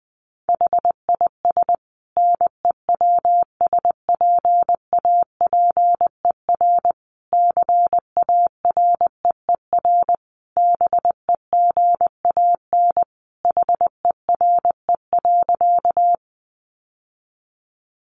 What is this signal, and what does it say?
Morse code, heard as a keyed tone, about 20 wpm, 700 Hz. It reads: HIS NEWSPAPER CAREER BEGAN HERE.
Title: Sentences from Top 1100 Words 20wpm